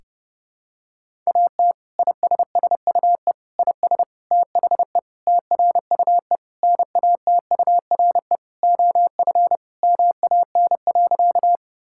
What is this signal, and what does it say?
Morse code, heard as a keyed tone, about 30 wpm, 700 Hz. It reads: AT ISSUE IS THE TRUE NATURE OF MAN.
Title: Sentences from Top 1000 Words - Review Part 6 - 30wpm